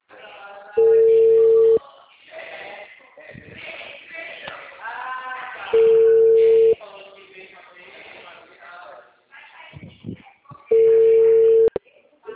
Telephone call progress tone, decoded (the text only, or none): ringback tone